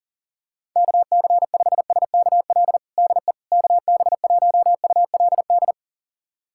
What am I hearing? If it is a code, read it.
Morse code, heard as a keyed tone, about 40 wpm, 700 Hz. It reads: KC5SKL DE KB1ULD
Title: USA Callsign DE Callsign - 6 Characters - 40wpm